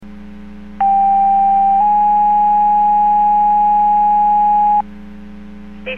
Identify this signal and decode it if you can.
Two-tone page, heard as a one-second tone, then a three-second tone, about 790 then 830 Hz. It is heard from 0.8 to 4.8 s.